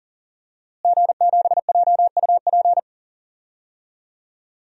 Morse code, heard as a keyed tone, about 40 wpm, 700 Hz. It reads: G7JUP
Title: DXCC-100 Call Signs - 5 Characters - Repeated 2x - 40wpm